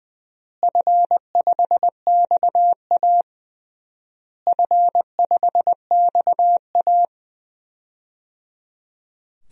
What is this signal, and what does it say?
Morse code, heard as a keyed tone, about 20 wpm, 700 Hz. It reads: F5XA F5XA
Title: DXCC-100 Call Signs - 4 Characters - Repeated 3x - 20wpm